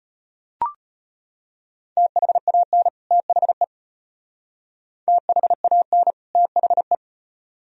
Morse code, heard as a keyed tone, about 32 wpm, 700 Hz. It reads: THAN THE THAN THE